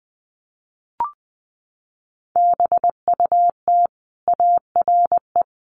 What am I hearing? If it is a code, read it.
Morse code, heard as a keyed tone, about 19 wpm, 700 Hz. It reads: BUT ARE
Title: Sentences from Top 300 Words 20wpm